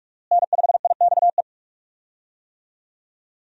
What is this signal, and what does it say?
Morse code, heard as a keyed tone, about 46 wpm, 700 Hz. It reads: N5IXE
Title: USA Call Signs - 5 Characters - Repeated 3x - 45wpm